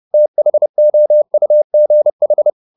Morse code, heard as a keyed tone, about 30 wpm, 600 Hz. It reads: THOUGH
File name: the old man and the sea / the old man and the sea 1 30wpm